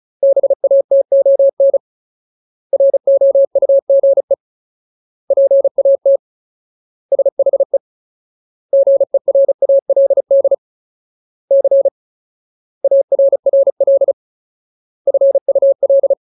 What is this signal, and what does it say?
Morse code, heard as a keyed tone, about 34 wpm, 550 Hz. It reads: BATON ROUGE PAT SHE GERALD C ARRL FUL
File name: Ultimate Word List / 35WPM Ultimate